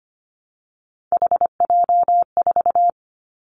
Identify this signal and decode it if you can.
Morse code, heard as a keyed tone, about 25 wpm, 700 Hz. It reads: HJ4
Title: Callsign Prefixes for Top 100 DXCC - 25wpm